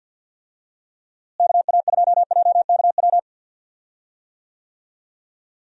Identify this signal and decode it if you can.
Morse code, heard as a keyed tone, about 48 wpm, 700 Hz. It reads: KA2JKW
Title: USA Call Signs - 6 Characters - Repeated 3x - 50wpm